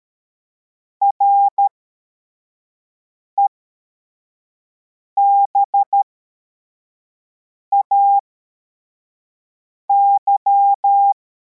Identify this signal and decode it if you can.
Morse code, heard as a keyed tone, about 13 wpm, spaced as for 4 wpm, 800 Hz. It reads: REBAY
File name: formation / cw / Text53